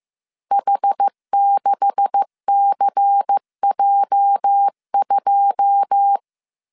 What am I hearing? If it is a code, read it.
Morse code, heard as a keyed tone, about 15 wpm, 800 Hz. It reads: H6CJ2
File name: CW / DET / 1 9 1500C S C 14wpm 3245